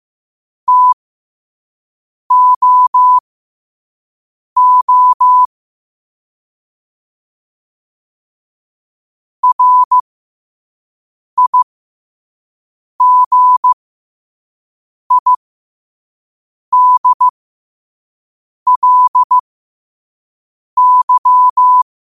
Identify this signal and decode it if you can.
Morse code, heard as a keyed tone, about 15 wpm, spaced as for 5 wpm, 1000 Hz. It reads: TOO RIGIDLY
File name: qotdfiles / 5wpm